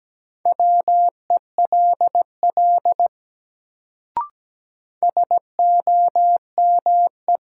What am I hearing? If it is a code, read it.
Morse code, heard as a keyed tone, about 17 wpm, 700 Hz. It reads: WELL SOME
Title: Sets of 2 Words - New Word Drill - 17wpm